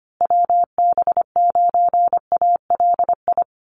Morse code, heard as a keyed tone, about 25 wpm, 700 Hz. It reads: WB9ALI
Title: USA Call Signs Repeated 3x 25wpm